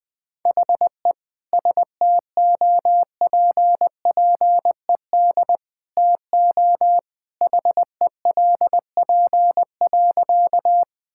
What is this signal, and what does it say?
Morse code, heard as a keyed tone, about 20 wpm, 700 Hz. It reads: HE STOPPED TO HELP.
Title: Sentences with 2-Word Phrases - 20wpm